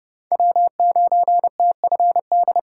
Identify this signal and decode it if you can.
Morse code, heard as a keyed tone, about 29 wpm, 700 Hz. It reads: W9TFD